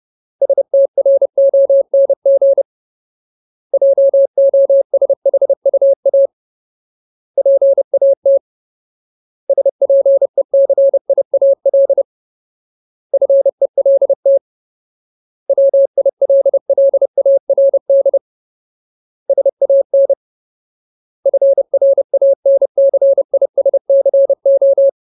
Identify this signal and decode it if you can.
Morse code, heard as a keyed tone, about 30 wpm, 550 Hz. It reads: STRONG JOSHUA PAT SPECIAL FELT WILLARD SAN FRANCISCO
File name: Ultimate Word List / 30WPM Ultimate